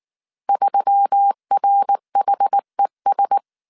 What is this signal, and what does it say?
Morse code, heard as a keyed tone, about 19 wpm, 800 Hz. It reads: 3RHES
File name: CW / DET / 1 6 1500C S A 18wpm 2255